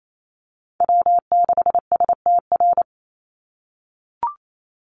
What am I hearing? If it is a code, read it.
Morse code, heard as a keyed tone, about 28 wpm, 700 Hz. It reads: W6STR
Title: Call Signs - 28wpm